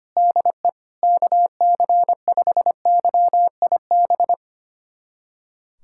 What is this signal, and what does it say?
Morse code, heard as a keyed tone, about 25 wpm, 700 Hz. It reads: DE KC5YIB
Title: USA Callsign DE Callsign - 6 Characters - 25wpm